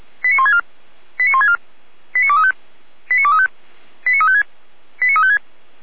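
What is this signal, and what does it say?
Five-tone selective call (ZVEI), heard as quick runs of five tones, tones of about 80 ms. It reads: 89164 89164 89235 89235 89356 89356
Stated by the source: ZVEI five-tone call 89350 received by SDR.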